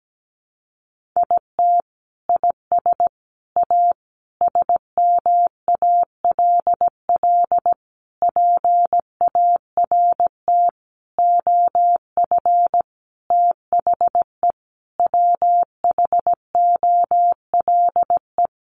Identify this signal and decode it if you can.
Morse code, heard as a keyed tone, about 17 wpm, 700 Hz. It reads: IT IS A SMALL PART OF THE WHOLE
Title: Sentences from Top 1500 Words 17wpm